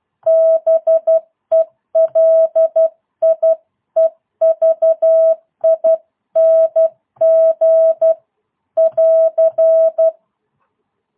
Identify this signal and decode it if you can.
Morse code, heard as a keyed tone, about 12 wpm, 650 Hz. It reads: BELIEVING <AR>
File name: jh8cbh / 2022cw